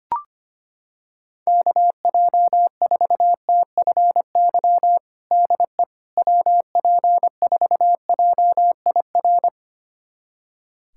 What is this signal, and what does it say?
Morse code, heard as a keyed tone, about 25 wpm, 700 Hz. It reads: KJ4TFY DE WP4JIR
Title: USA Callsign DE Callsign 25wpm